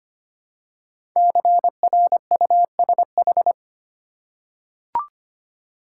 Morse code, heard as a keyed tone, about 25 wpm, 700 Hz. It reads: CRUSH
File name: 5 Letter Words - First 3 Letters Given - 25wpm